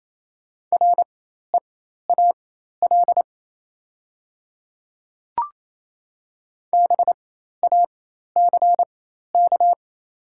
Morse code, heard as a keyed tone, about 28 wpm, spaced as for 13 wpm, 700 Hz. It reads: REAL BACK
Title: 4x Character Spacing - Top 500 Words - 28wpm